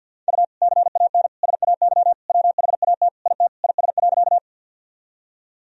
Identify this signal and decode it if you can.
Morse code, heard as a keyed tone, about 50 wpm, 700 Hz. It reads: U CAN SAY WHAT IT IS.
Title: Sentences from Top 200 Words and Abbreviations 50wpm